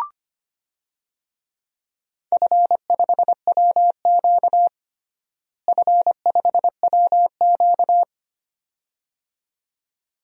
Morse code, heard as a keyed tone, about 25 wpm, 700 Hz. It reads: F5WQ F5WQ
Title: DXCC-100 Call Signs - 4 Characters - Repeated 2x - 25wpm